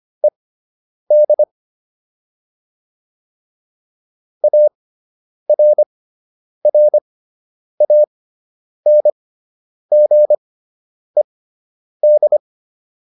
Morse code, heard as a keyed tone, about 25 wpm, spaced as for 9 wpm, 600 Hz. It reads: ED ARRANGED